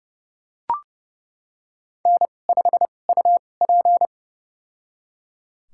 Morse code, heard as a keyed tone, about 32 wpm, spaced as for 22 wpm, 700 Hz. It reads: N5UP
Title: USA Call Signs - 2x Character Spacing - 30wpm